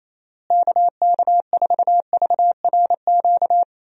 Morse code, heard as a keyed tone, about 28 wpm, 700 Hz. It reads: KK4VRQ
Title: USA Call Signs - 6 Characters - Repeated 3x - 28wpm